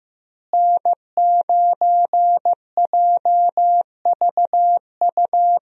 Morse code, heard as a keyed tone, about 15 wpm, 700 Hz. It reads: N9JVU
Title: USA Call Signs - 5 Characters - Repeated 2x - 15wpm